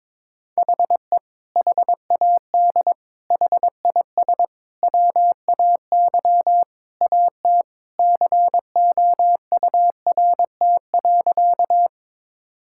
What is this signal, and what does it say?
Morse code, heard as a keyed tone, about 22 wpm, 700 Hz. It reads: HE HAD HIS WAY AT COURT.
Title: Sentences from Top 1600 Words 22wpm